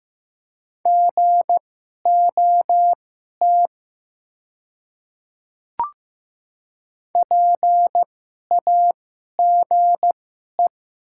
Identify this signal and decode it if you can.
Morse code, heard as a keyed tone, about 15 wpm, spaced as for 11 wpm, 700 Hz. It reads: GOT PAGE
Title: 2x Character Spacing - Top 300 Words - 15wpm